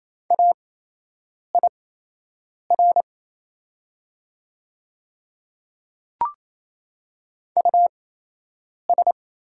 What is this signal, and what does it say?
Morse code, heard as a keyed tone, about 28 wpm, spaced as for 8 wpm, 700 Hz. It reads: AIR US
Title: 8x Character Spacing - Top 200 Words - 28wpm